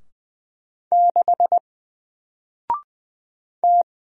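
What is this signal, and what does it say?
Morse code, heard as a keyed tone, about 20 wpm, 700 Hz. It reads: 6 T
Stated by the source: Single letter and number Morse drill at 20 wpm, from N4LCD.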